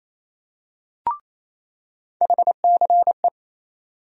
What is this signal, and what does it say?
Morse code, heard as a keyed tone, about 28 wpm, 700 Hz. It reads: HCE
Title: USA Call Sign Postfixes - 3 Letters 28wpm